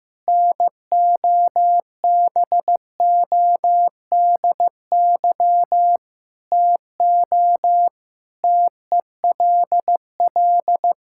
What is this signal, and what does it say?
Morse code, heard as a keyed tone, about 16 wpm, 700 Hz. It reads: NOBODY TO TELL
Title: Sentences from Top 1600 Words 15wpm